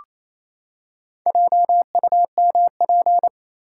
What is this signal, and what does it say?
Morse code, heard as a keyed tone, about 27 wpm, 700 Hz. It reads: JUMP
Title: Four Letter Words - 28wpm